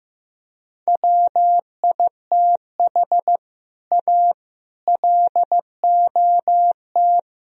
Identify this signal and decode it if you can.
Morse code, heard as a keyed tone, about 16 wpm, 700 Hz. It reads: WITH A LOT
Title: Speed-Racing - Sets of 4 Words 15wpm